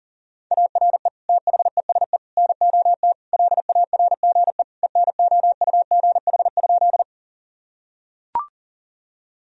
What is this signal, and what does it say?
Morse code, heard as a keyed tone, about 40 wpm, 700 Hz. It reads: ARE THESE NOT LARGE ENOUGH?
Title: Sentences from Top 1000 Words - Review Part 1 - 40wpm